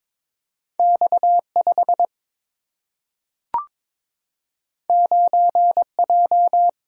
Morse code, heard as a keyed tone, about 22 wpm, 700 Hz. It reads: X 5 9 J
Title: Single-Letter Single-Number 22wpm